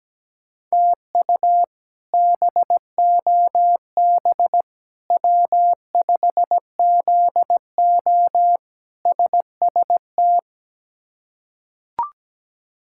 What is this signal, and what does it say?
Morse code, heard as a keyed tone, about 17 wpm, 700 Hz. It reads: TU BOB W5ZO SST
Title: K1USN SST QSO Practice - 17wpm